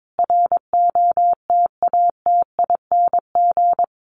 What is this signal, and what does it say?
Morse code, heard as a keyed tone, about 22 wpm, 700 Hz. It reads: ROTATING
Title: Additional Easy Phrases 22wpm